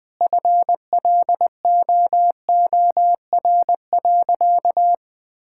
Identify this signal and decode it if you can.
Morse code, heard as a keyed tone, about 20 wpm, 700 Hz. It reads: FLOOR.
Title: Easy Sentences with Hints - 20wpm